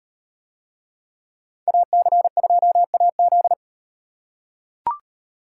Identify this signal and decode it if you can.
Morse code, heard as a keyed tone, about 35 wpm, 700 Hz. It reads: AC2AZ